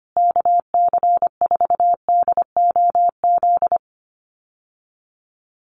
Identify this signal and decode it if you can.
Morse code, heard as a keyed tone, about 25 wpm, 700 Hz. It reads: KC4DOZ
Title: USA Call Signs Repeated 3x 25wpm